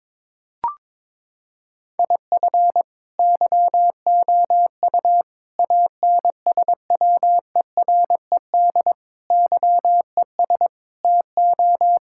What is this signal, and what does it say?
Morse code, heard as a keyed tone, about 22 wpm, 700 Hz. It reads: IF YOU ANSWERED YES TO
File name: Sentences with 9-Word Phrases - 22wpm